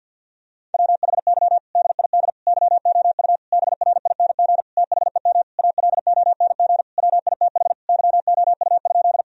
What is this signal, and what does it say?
Morse code, heard as a keyed tone, about 51 wpm, 700 Hz. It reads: WHY DID YOU BRING THEM ALONG WITH YOU?
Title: Sentences from Top 1000 Words - Review Part 3 - 50wpm